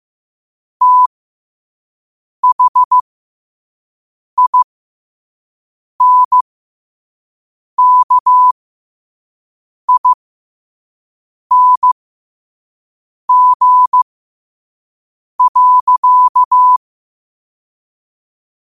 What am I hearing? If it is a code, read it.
Morse code, heard as a keyed tone, about 15 wpm, spaced as for 5 wpm, 1000 Hz. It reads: THINKING.
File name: qotdfiles / 5wpm